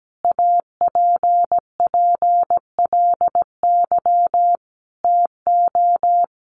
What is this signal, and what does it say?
Morse code, heard as a keyed tone, about 17 wpm, 700 Hz. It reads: APPLY TO
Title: Sentences from Top 1200 Words 17wpm